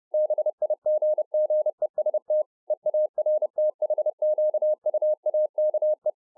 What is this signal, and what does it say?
Morse code, heard as a keyed tone, about 30 wpm, 610 Hz. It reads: BIGGEST EARTHQUAKE